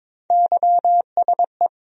Morse code, heard as a keyed tone, about 21 wpm, 700 Hz. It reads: YSE
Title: USA Call Sign Postfixes - 3 Letters 22wpm